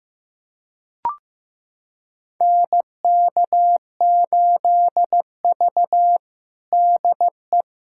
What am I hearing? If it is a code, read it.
Morse code, heard as a keyed tone, about 16 wpm, 700 Hz. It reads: NK8V DE
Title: USA Callsign DE Callsign - 4 Characters - 15wpm